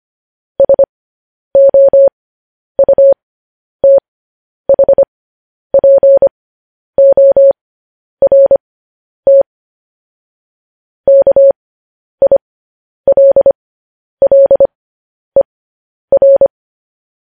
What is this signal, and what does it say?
Morse code, heard as a keyed tone, about 25 wpm, spaced as for 10 wpm, 550 Hz. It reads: SOUTHPORT KILLER
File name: podcast / News headlines-1x-25x10